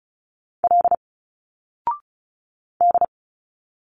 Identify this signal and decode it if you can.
Morse code, heard as a keyed tone, about 34 wpm, 700 Hz. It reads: L D
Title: Single Letters 35wpm